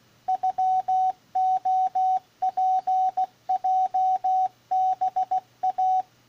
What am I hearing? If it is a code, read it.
Morse code (Japanese Wabun code), heard as a keyed tone, about 16 wpm, 750 Hz, with radio noise. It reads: ノレツヲハイ